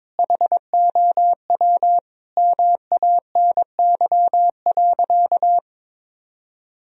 Morse code, heard as a keyed tone, about 22 wpm, 700 Hz. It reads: HOW MANY.
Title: Sentences from Top 1000 Words - Review Part 1 - 22wpm